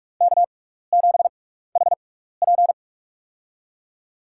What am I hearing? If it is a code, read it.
Morse code, heard as a keyed tone, about 46 wpm, spaced as for 16 wpm, 700 Hz. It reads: K7HP
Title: USA Call Signs - 4 Characters - 6x Character Spacing - 45wpm